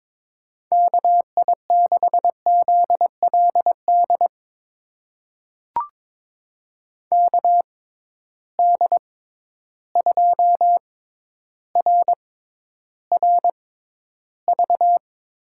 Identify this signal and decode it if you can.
Morse code, heard as a keyed tone, about 21 wpm, spaced as for 11 wpm, 700 Hz. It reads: KI6ZLD KD2RRV
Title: USA Call Signs - 6x Character Spacing - 22wpm